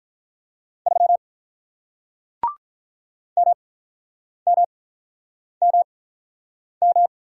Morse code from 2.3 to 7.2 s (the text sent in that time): MMMM